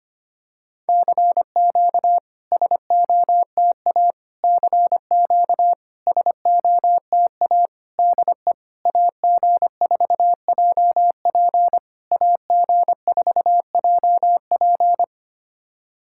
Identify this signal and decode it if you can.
Morse code, heard as a keyed tone, about 23 wpm, 700 Hz. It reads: CQ SOTA CQ SOTA DE AG4JP AG4JP
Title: SOTA QSO Practice 25wpm